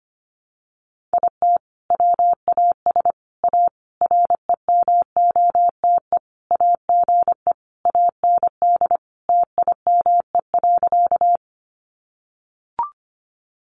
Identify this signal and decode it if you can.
Morse code, heard as a keyed tone, about 25 wpm, 700 Hz. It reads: IT WAS A REMOTE AGE AND TIME.
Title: Sentences from Top 1300 Words 25wpm